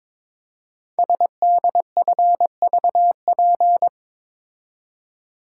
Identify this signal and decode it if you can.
Morse code, heard as a keyed tone, about 22 wpm, 700 Hz. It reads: SDFVP
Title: Code Groups - 5 Letter:Numbers - 22wpm